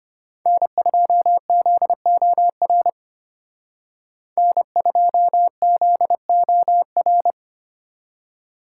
Morse code, heard as a keyed tone, about 31 wpm, 700 Hz. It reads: N2ZOR N2ZOR